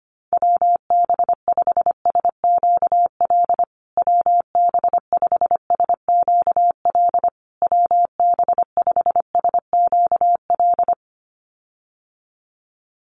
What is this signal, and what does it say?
Morse code, heard as a keyed tone, about 25 wpm, 700 Hz. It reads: WB5SQL WB5SQL WB5SQL
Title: USA Call Signs Repeated 3x 25wpm